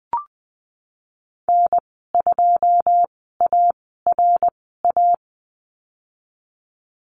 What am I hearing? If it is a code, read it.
Morse code, heard as a keyed tone, about 20 wpm, spaced as for 15 wpm, 700 Hz. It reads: N2ARA
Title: 088 Lesson 15g - Number 5 - Call Signs 20wpm